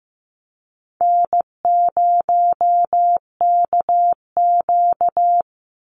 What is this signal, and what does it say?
Morse code, heard as a keyed tone, about 14 wpm, 700 Hz. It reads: N0KQ
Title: Easy Call Signs Repeated 2x - 15wpm